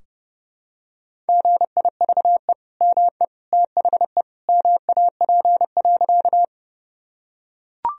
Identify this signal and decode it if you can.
Morse code, heard as a keyed tone, about 28 wpm, 700 Hz. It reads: GIVE ME THE MAP.
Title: Sentences from Top 1000 Words - Review Part 2 - 30wpm